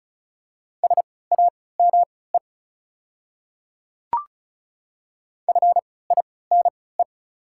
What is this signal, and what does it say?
Morse code, heard as a keyed tone, about 34 wpm, spaced as for 20 wpm, 690 Hz. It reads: SAME FINE
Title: 3x Character Spacing - Top 500 Words - 35wpm